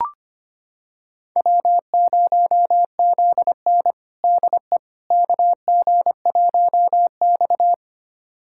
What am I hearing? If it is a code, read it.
Morse code, heard as a keyed tone, about 25 wpm, 700 Hz. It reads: W0ZN DE KG1X
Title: USA Callsign DE Callsign - 4 Characters - 25wpm